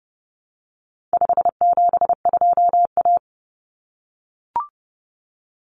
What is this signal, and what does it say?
Morse code, heard as a keyed tone, about 30 wpm, 700 Hz. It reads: 572A
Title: Common Cut Numbers - 4-Digit 30wpm